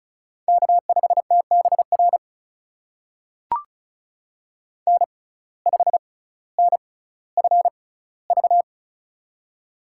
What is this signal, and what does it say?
Morse code, heard as a keyed tone, about 36 wpm, spaced as for 18 wpm, 700 Hz. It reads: K5TBR N5NFV
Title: USA Call Signs - 5 Characters - 6x Character Spacing - 35wpm